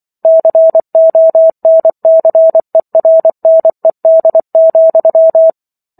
Morse code, heard as a keyed tone, about 24 wpm, 650 Hz. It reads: CONCERNED,